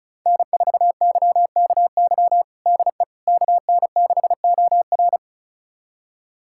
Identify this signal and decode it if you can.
Morse code, heard as a keyed tone, about 34 wpm, 700 Hz. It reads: N4YKY DE KN6OR
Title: USA Callsign DE Callsign 35wpm